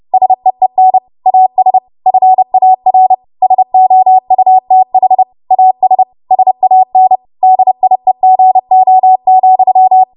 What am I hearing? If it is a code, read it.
Morse code, heard as a keyed tone, about 30 wpm, 750 Hz. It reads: SEEN AS FAR SOUTH AS SAN DIEGO,
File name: Morse / 30WPM